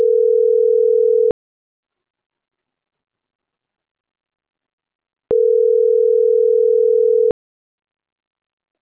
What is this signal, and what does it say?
Ringback tone (North America), 440 + 480 Hz.